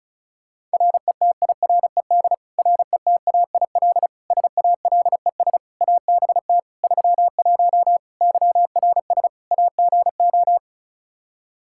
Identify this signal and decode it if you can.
Morse code, heard as a keyed tone, about 36 wpm, 700 Hz. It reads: RETIRED RETAIL SALES ABT 31 YRS AGO
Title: Practice QSO - Exchange 3 - More Info - 35wpm